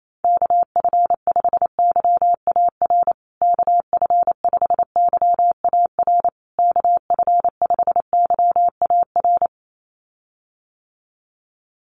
Morse code, heard as a keyed tone, about 28 wpm, 700 Hz. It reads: KF5YAR KF5YAR KF5YAR
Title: USA Call Signs Repeated 3x 28wpm